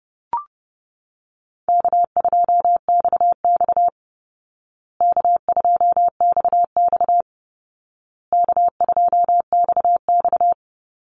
Morse code, heard as a keyed tone, about 28 wpm, 700 Hz. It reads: K2XX K2XX K2XX
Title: Easy Call Signs Repeated 3x - 30wpm